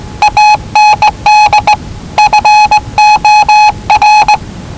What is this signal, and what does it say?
Morse code, heard as a keyed tone, about 19 wpm, 840 Hz. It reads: AND FOR
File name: WellingtonRadioZlwCloseDownOnHfAndMf / 01-Zlw 4277khz 417.5 qrt until 0925z